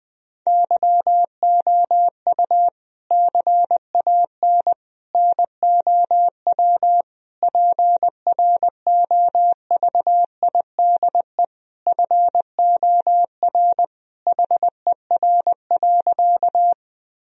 Morse code, heard as a keyed tone, about 20 wpm, 700 Hz. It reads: YOU CAN NOW PROVIDE FOR HER.
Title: Sentences from Top 1000 Words - Review Part 3 - 20wpm